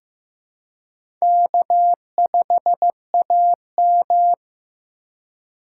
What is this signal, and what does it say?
Morse code, heard as a keyed tone, about 15 wpm, 700 Hz. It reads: K5AM
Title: K1USN SST QSO Practice - 15wpm